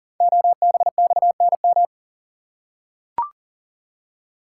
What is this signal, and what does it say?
Morse code, heard as a keyed tone, about 42 wpm, 700 Hz. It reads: OBXNM
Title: Code Groups - 5 Letters - 40wpm